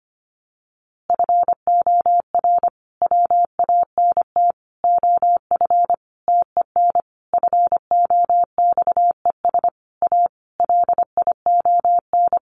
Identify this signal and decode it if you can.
Morse code, heard as a keyed tone, about 25 wpm, 700 Hz. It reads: FOR WANT OF TEN FOXES A LION
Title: A Sound Of Thunder 25wpm